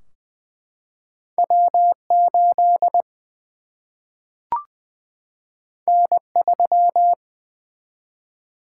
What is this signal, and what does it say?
Morse code, heard as a keyed tone, about 20 wpm, 700 Hz. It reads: W 8 N 3
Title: Single-Letter Single-Number 20wpm